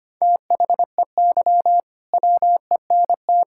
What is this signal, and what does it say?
Morse code, heard as a keyed tone, about 25 wpm, 700 Hz. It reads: THEY WENT
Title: Sentences from Top 1600 Words 25wpm